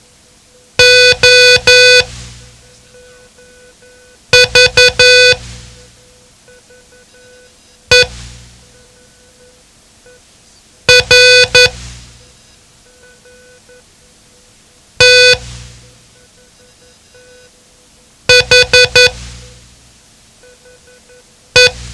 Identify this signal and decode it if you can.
Morse code, heard as a keyed tone, about 11 wpm, spaced as for 3 wpm, 500 Hz. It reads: OVER THE
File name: ham radio / CodeQuick / Code Quick 07